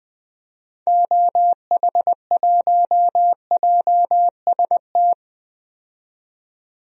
Morse code, heard as a keyed tone, about 21 wpm, 700 Hz. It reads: OH1JST
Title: DXCC-100 Call Signs - 6 Characters - 20wpm